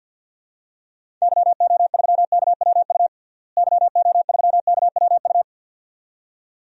Morse code, heard as a keyed tone, about 52 wpm, 700 Hz. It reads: YO3KWU YO3KWU
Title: DXCC-100 Call Signs - 6 Characters - Repeated 2x - 50wpm